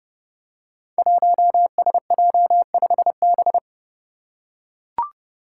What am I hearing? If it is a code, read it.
Morse code, heard as a keyed tone, about 30 wpm, 700 Hz. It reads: 1SJ5B
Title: Five Challenging Letter-Number Combinations 30wpm